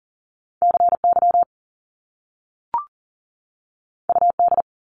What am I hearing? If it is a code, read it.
Morse code, heard as a keyed tone, about 41 wpm, 700 Hz. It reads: CY UD
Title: Top Two Letter Combinations in Words 40wpm